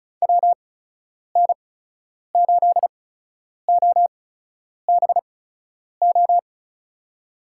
Morse code, heard as a keyed tone, about 34 wpm, spaced as for 10 wpm, 700 Hz. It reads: WN8OBO
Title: USA Call Signs - 6 Characters - 8x Character Spacing - 35wpm